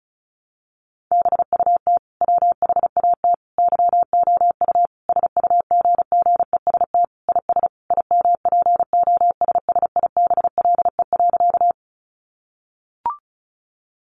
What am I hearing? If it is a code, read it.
Morse code, heard as a keyed tone, about 34 wpm, 700 Hz. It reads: BUT WHAT YOU SUGGEST IS IMPOSSIBLE.
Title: Sentences from Top 1300 Words 35wpm